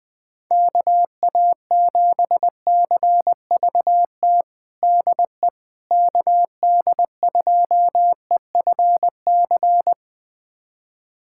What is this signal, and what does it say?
Morse code, heard as a keyed tone, about 20 wpm, 700 Hz. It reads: KA7CVT DE KD2EFC
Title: USA Callsign DE Callsign 20wpm